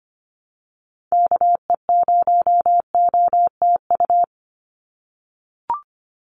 Morse code, heard as a keyed tone, about 25 wpm, 700 Hz. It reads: KE0OTU
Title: USA Call Signs 25wpm